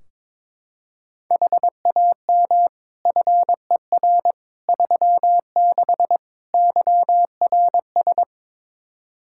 Morse code, heard as a keyed tone, about 22 wpm, 700 Hz. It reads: HAM FER 36 YRS
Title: Practice QSO - Exchange 3 - More Info - 22wpm